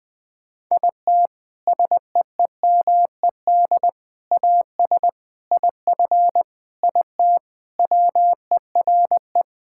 Morse code, heard as a keyed tone, about 20 wpm, 700 Hz. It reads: IT SEEMED AS IF IT WERE
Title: Sentences with 4-Word Phrases - 20wpm